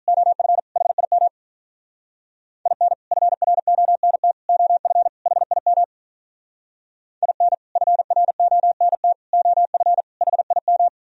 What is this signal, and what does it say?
Morse code, heard as a keyed tone, about 43 wpm, 700 Hz. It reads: OF HIM IN FRONT OF HIM IN FRONT OF HIM
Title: Speed-Racing - Sets of 4 Words 35wpm